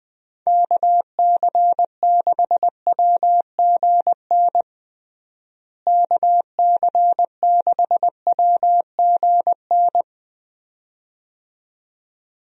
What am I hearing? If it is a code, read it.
Morse code, heard as a keyed tone, about 20 wpm, 700 Hz. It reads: KC6WGN KC6WGN
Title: USA Call Signs - 6 Characters - Repeated 2x - 20wpm